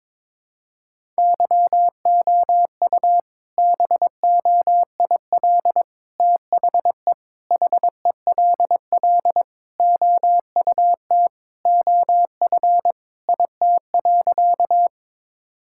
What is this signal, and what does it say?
Morse code, heard as a keyed tone, about 22 wpm, 700 Hz. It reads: YOU BOIL THE HELL OUT OF IT.
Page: morse-code-ninja-practice-22wpm